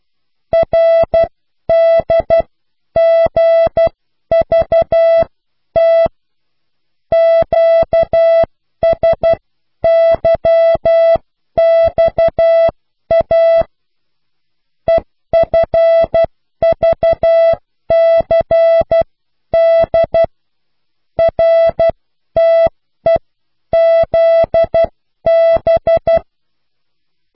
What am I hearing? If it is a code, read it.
Morse code, heard as a keyed tone, about 12 wpm, spaced as for 10 wpm, 660 Hz. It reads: RDGVT QSYXA EFVCD RTEZB